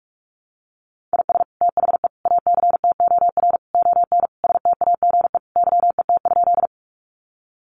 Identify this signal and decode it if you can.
Morse code, heard as a keyed tone, about 44 wpm, 700 Hz. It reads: IS THE ACTOR ON STAGE YET?
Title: Sentences from Top 1500 Words 45wpm